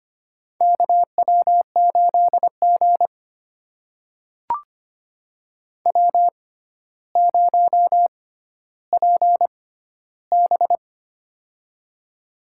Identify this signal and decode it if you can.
Morse code, heard as a keyed tone, about 25 wpm, spaced as for 13 wpm, 700 Hz. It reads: KW8G W0PB
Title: USA Call Signs - 4 Characters - 6x Character Spacing - 25wpm